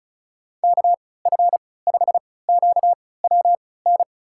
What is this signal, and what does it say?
Morse code, heard as a keyed tone, about 34 wpm, spaced as for 20 wpm, 700 Hz. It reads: KF5QWN
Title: USA Call Signs - 6 Characters - 3x Character Spacing - 35wpm